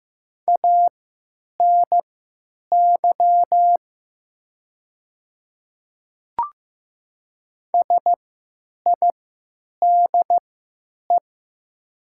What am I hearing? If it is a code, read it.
Morse code, heard as a keyed tone, about 15 wpm, spaced as for 9 wpm, 700 Hz. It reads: ANY SIDE
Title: 3x Character Spacing - Top 300 Words - 15wpm